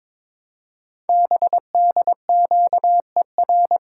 Morse code, heard as a keyed tone, about 23 wpm, 700 Hz. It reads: BDQER
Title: Code Groups - 5 Letters - 22wpm